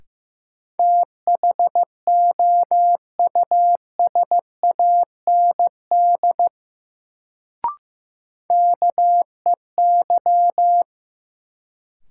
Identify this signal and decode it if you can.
Morse code, heard as a keyed tone, about 15 wpm, 700 Hz. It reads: THOUSAND KEY
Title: Sentences from Top 600 Words - New Word Drill - 15wpm